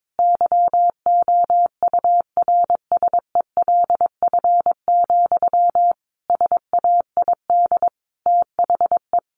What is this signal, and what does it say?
Morse code, heard as a keyed tone, about 22 wpm, 700 Hz. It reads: YOURSELF, SAID THE
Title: Sentences from Top 1100 Words 22wpm